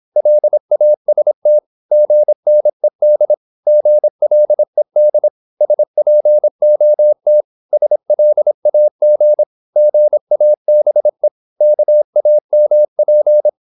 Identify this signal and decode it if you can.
Morse code, heard as a keyed tone, about 26 wpm, 600 Hz. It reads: LAST GNED GLED SPOT SLAG GABE KAMP
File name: Korte ord DK 26wpm